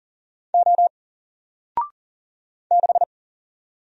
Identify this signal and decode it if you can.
Morse code, heard as a keyed tone, about 40 wpm, 700 Hz. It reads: O 6
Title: Single Character (Letter:Number:Punctuation:Prosign) Alt 40wpm